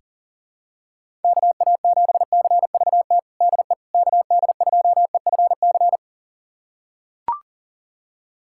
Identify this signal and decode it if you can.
Morse code, heard as a keyed tone, about 40 wpm, 700 Hz. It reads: KA7CVT DE KD2EFC
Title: USA Callsign DE Callsign 40wpm